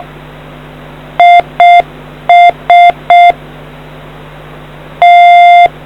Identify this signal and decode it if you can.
Morse code, heard as a keyed tone, about 6 wpm, 700 Hz. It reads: IS T